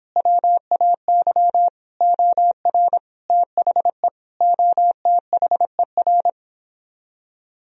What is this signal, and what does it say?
Morse code, heard as a keyed tone, about 26 wpm, 700 Hz. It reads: WAY OR THE OTHER